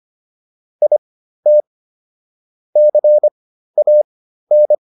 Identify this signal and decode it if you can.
Morse code, heard as a keyed tone, about 25 wpm, spaced as for 13 wpm, 600 Hz. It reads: IT CAN
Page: intermediate-practice-files